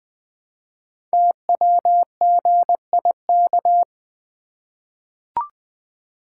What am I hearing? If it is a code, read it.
Morse code, heard as a keyed tone, about 20 wpm, 700 Hz. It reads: TWGIK
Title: Code Groups - 5 Letters - 20wpm